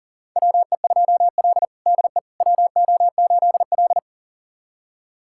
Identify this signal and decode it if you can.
Morse code, heard as a keyed tone, about 40 wpm, 700 Hz. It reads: WE2L DE WO8L
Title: USA Callsign DE Callsign - 4 Characters - 40wpm